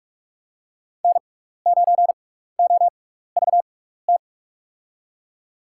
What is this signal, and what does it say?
Morse code, heard as a keyed tone, about 48 wpm, spaced as for 16 wpm, 700 Hz. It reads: N9OVT
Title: USA Call Signs - 5 Characters - 6x Character Spacing - 45wpm